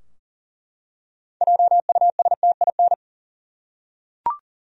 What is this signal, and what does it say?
Morse code, heard as a keyed tone, about 38 wpm, 700 Hz. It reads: JUSTIN
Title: USA Common Names - 40wpm